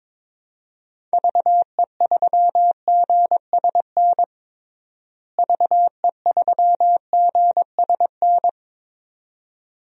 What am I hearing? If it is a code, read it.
Morse code, heard as a keyed tone, about 22 wpm, 700 Hz. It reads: VE3GSN VE3GSN
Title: DXCC-100 Call Signs - 6 Characters - Repeated 2x - 22wpm